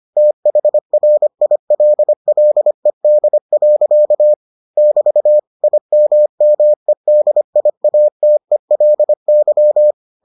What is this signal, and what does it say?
Morse code, heard as a keyed tone, about 25 wpm, 600 Hz. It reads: THRILLED. <BT> IMMEDIATELY